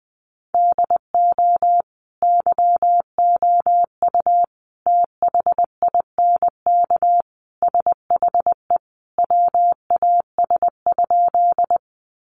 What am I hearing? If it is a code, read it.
Morse code, heard as a keyed tone, about 20 wpm, 700 Hz. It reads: DO YOU THINK SHE WAS?
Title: Sentences from Top 200 Words 20wpm